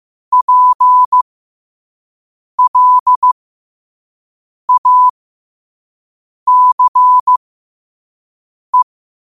Morse code, heard as a keyed tone, about 15 wpm, spaced as for 5 wpm, 1000 Hz. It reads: PLACE
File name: qotdfiles / 5wpm